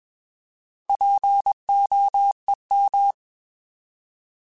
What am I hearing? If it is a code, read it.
Morse code, heard as a keyed tone, about 21 wpm, 800 Hz. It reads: POEM
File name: Alphabetical / Alphabetical mp3s / Ps @20